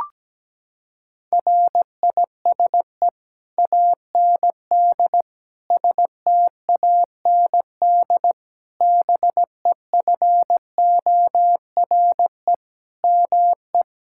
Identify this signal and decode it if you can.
Morse code, heard as a keyed tone, about 17 wpm, 700 Hz. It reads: RISE AND STAND BEFORE ME
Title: Sentences from Top 1600 Words 17wpm